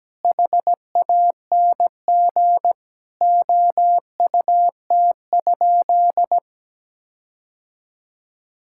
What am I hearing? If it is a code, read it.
Morse code, heard as a keyed tone, about 17 wpm, 700 Hz. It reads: HANG OUT?
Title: Sentences from Top 1600 Words 17wpm